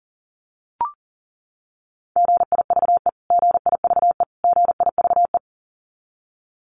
Morse code, heard as a keyed tone, about 40 wpm, 700 Hz. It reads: GIVE GIVE GIVE
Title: Top 500 Words Repeated 40wpm